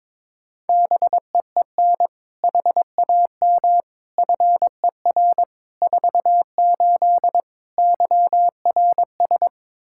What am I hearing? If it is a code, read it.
Morse code, heard as a keyed tone, about 22 wpm, 700 Hz. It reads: BEEN HAM FER 48 YRS
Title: Practice QSO - Exchange 3 - More Info - 22wpm